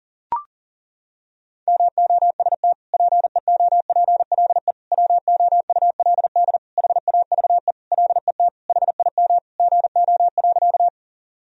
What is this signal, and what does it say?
Morse code, heard as a keyed tone, about 40 wpm, 700 Hz. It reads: MOST PEOPLE WOULD HAVE LET HIM GO.
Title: Sentences from Top 1000 Words - Review Part 1 - 40wpm